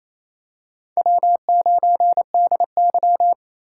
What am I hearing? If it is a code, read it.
Morse code, heard as a keyed tone, about 28 wpm, 700 Hz. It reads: W9DY
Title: K1USN SST QSO Practice - 28wpm